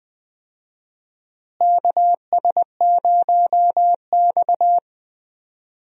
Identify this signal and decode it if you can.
Morse code, heard as a keyed tone, about 20 wpm, 700 Hz. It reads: KS0X
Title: USA Call Signs - 4 Characters - Repeated 2x - 20wpm